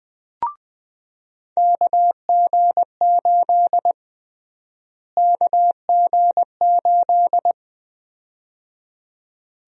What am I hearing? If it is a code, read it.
Morse code, heard as a keyed tone, about 21 wpm, 700 Hz. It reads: KG8 KG8
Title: USA Call Sign Prefixes - Repeated 2x - 20wpm